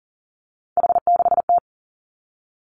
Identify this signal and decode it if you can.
Morse code, heard as a keyed tone, about 38 wpm, 700 Hz. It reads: H6T
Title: Three Challenging Letter-Number Combinations 40wpm